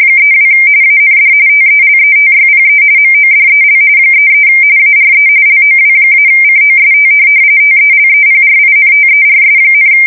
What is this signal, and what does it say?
RTTY, about 45.45 baud. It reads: 32 AND TELETYPE MODEL 33 ARE LOW-COST TELEPRINTERS,
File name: fsk170 / teletype-corp